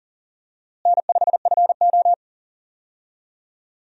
Morse code, heard as a keyed tone, about 40 wpm, 700 Hz. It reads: N5FO
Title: USA Call Signs - 4 Characters - Repeated 2x - 40wpm